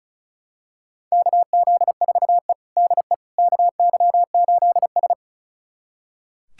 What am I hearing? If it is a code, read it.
Morse code, heard as a keyed tone, about 36 wpm, 700 Hz. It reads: KZ4E DE KY8S
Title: USA Callsign DE Callsign - 4 Characters - 35wpm